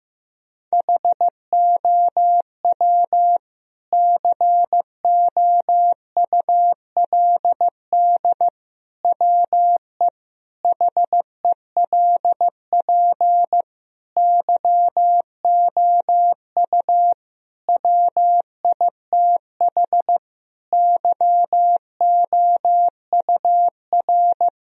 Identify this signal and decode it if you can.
Morse code, heard as a keyed tone, about 15 wpm, 700 Hz. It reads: HOW COULD WE HELP YOU WITH YOUR
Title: Sentences from Top 1000 Words - Review Part 3 - 15wpm